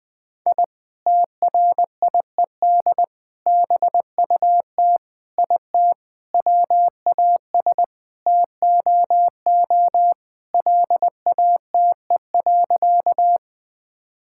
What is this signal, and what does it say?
Morse code, heard as a keyed tone, about 21 wpm, 700 Hz. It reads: I TRIED BUT IT WAS TOO LATE.
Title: Sentences with 3-Word Phrases - Spoken Beginning - 20wpm